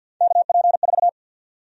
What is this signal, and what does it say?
Morse code, heard as a keyed tone, about 52 wpm, 700 Hz. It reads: KP4
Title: USA Call Sign Prefixes - 50wpm